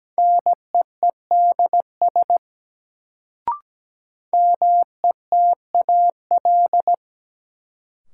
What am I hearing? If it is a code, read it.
Morse code, heard as a keyed tone, about 17 wpm, 700 Hz. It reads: NEEDS METAL
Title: Sentences from Top 1600 Words - New Word Drill Alt - 17wpm